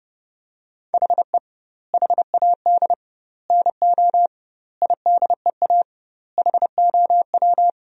Morse code, heard as a keyed tone, about 30 wpm, 700 Hz. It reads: HE HAD NO IDEA HOW
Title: 2x Word-Spacing - Sentences from Top 300 Words - 30wpm